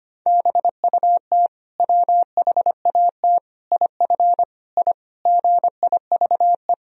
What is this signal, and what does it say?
Morse code, heard as a keyed tone, about 25 wpm, 700 Hz. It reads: BUT WHAT IF I GIVE
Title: Sentences from Top 1000 Words - Review Part 2 - 25wpm